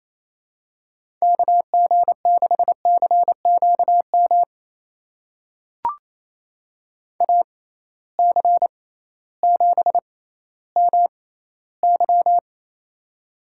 Morse code, heard as a keyed tone, about 28 wpm, spaced as for 15 wpm, 700 Hz. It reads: KG6CQM AC7MY
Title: USA Call Signs - 6x Character Spacing - 28wpm